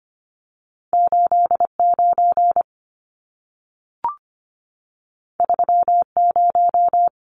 Morse code, heard as a keyed tone, about 25 wpm, 700 Hz. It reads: 89 30
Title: Numbers - 2-Digit 25wpm